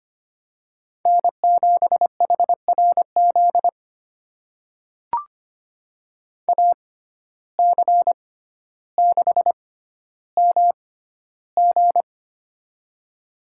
Morse code, heard as a keyed tone, about 25 wpm, spaced as for 13 wpm, 700 Hz. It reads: N7HRZ AC6MG
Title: USA Call Signs - 5 Characters - 6x Character Spacing - 25wpm